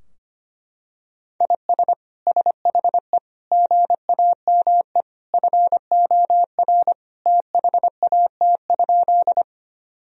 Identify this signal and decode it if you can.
Morse code, heard as a keyed tone, about 25 wpm, 700 Hz. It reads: IS SHE GAME FOR THAT?
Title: Sentences from Top 1000 Words - Review Part 3 - 25wpm